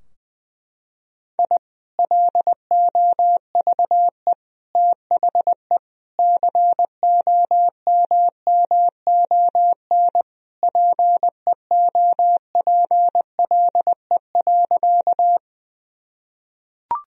Morse code, heard as a keyed tone, about 21 wpm, 700 Hz. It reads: I LOVE THE COMMON PEOPLE.
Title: Sentences from Top 1000 Words - Review Part 3 - 20wpm